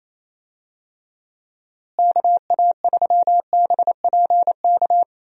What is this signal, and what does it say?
Morse code, heard as a keyed tone, about 28 wpm, 700 Hz. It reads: KA3BPK
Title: USA Call Signs - 6 Characters - Repeated 3x - 28wpm